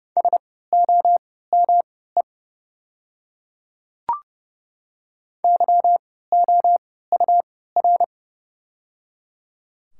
Morse code, heard as a keyed tone, about 30 wpm, spaced as for 17 wpm, 700 Hz. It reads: SOME YOUR
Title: 3x Character Spacing - Top 100 Words - 30wpm